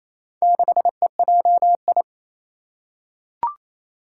Morse code, heard as a keyed tone, about 27 wpm, 700 Hz. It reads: 6EJI
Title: Four Challenging Letter-Number Combinations 28wpm